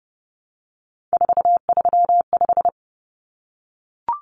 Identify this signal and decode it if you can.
Morse code, heard as a keyed tone, about 30 wpm, 700 Hz. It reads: 435
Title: Common Cut Numbers - 3-Digit 30wpm